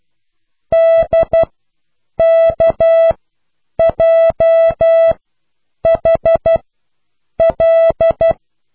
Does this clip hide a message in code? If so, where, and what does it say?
Yes: Morse code from 0.7 to 8.3 s, heard as a keyed tone, about 12 wpm, spaced as for 8 wpm, 660 Hz. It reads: DKJHL